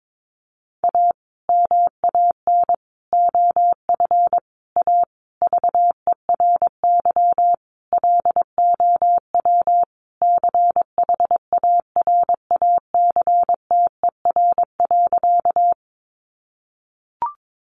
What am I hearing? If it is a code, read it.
Morse code, heard as a keyed tone, about 22 wpm, 700 Hz. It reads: A MAN OF A VERY LOW CHARACTER.
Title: Sentences from Top 1200 Words 22wpm